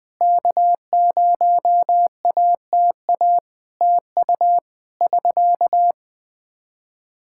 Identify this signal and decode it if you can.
Morse code, heard as a keyed tone, about 21 wpm, 700 Hz. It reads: K0ATA TU <SK>
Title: Practice QSO - Exchange 4 - Ending - 20wpm